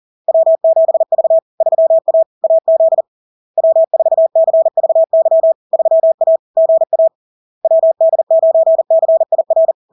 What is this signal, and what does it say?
Morse code, heard as a keyed tone, about 40 wpm, 650 Hz. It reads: W7V 3A AZ W4CVY 3A GA WD9CIR
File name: LARK-FD-2022-Log-in-CW-40wpm-650-Hz